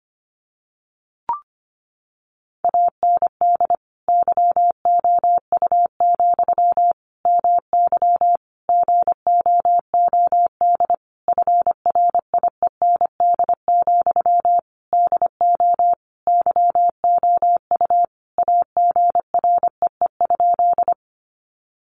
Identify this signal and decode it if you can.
Morse code, heard as a keyed tone, about 25 wpm, 700 Hz. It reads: AND YOU, MY GOOD FRIEND, DO YOU AGREE?
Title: Sentences from Top 1100 Words 25wpm